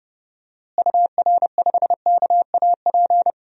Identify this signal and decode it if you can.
Morse code, heard as a keyed tone, about 32 wpm, 700 Hz. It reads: UR5KAP
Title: DXCC-100 Call Signs - 6 Characters - Repeated 3x - 30wpm